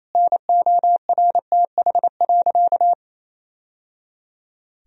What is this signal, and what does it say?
Morse code, heard as a keyed tone, about 28 wpm, 700 Hz. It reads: NORTH.
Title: Sentences from Top 1000 Words - Review Part 1 - 28wpm